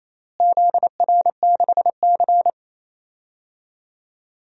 Morse code, heard as a keyed tone, about 28 wpm, 700 Hz. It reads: ZR6C
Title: DXCC-100 Call Signs - 4 Characters - Repeated 3x - 28wpm